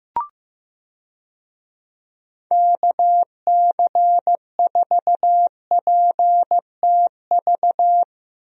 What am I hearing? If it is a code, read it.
Morse code, heard as a keyed tone, about 15 wpm, 700 Hz. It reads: KC4PTV
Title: USA Call Signs - 6 Characters - Repeated 2x - 15wpm